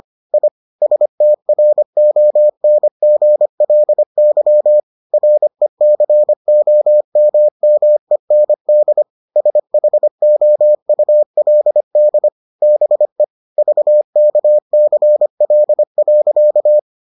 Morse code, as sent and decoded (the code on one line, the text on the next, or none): .. / ... - .-. --- -. --. .-.. -.-- / .-. -.-. --- -- -- . -. -.. / ... .... --- ..- .-.. -.. / -... / ...- -.- -.-. .-.. .-.-.-
I STRONGLY RECOMMEND SHOULD BE VKCL.